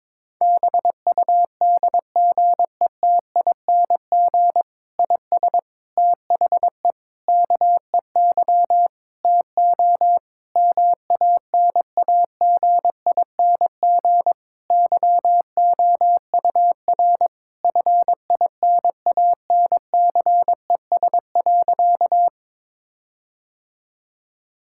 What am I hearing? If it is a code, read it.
Morse code, heard as a keyed tone, about 22 wpm, 700 Hz. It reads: BUDGETING IS THE KEY TO MANAGING YOUR FINANCES.
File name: Difficult Sentences with Hints - 22wpm